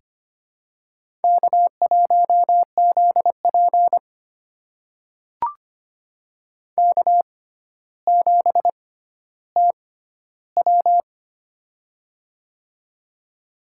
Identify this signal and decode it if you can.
Morse code, heard as a keyed tone, about 25 wpm, spaced as for 13 wpm, 700 Hz. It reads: K1ZP K7TW
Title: USA Call Signs - 4 Characters - 6x Character Spacing - 25wpm